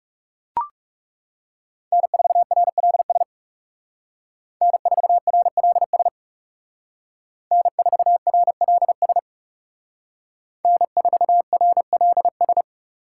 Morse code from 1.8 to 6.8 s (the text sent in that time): N4RLS N4RLS